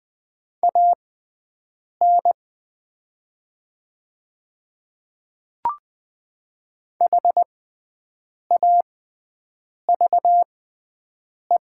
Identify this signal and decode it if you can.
Morse code, heard as a keyed tone, about 20 wpm, spaced as for 7 wpm, 700 Hz. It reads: AN HAVE
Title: 6x Character Spacing - Top 100 Words - 20wpm